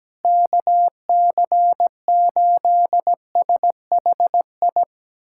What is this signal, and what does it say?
Morse code, heard as a keyed tone, about 17 wpm, 700 Hz. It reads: KC8SHI
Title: USA Callsign DE Callsign - 6 Characters - 17wpm